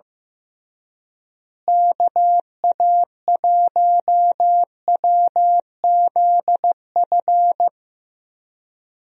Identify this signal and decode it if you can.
Morse code, heard as a keyed tone, about 15 wpm, 700 Hz. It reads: KA1WZF
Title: USA Call Signs - 6 Characters - Repeated 3x - 15wpm